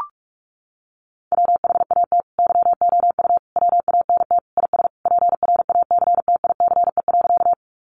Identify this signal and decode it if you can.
Morse code, heard as a keyed tone, about 44 wpm, 700 Hz. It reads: WHAT YOU WANT IS PRACTICE.
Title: Sentences from Top 700 Words 45wpm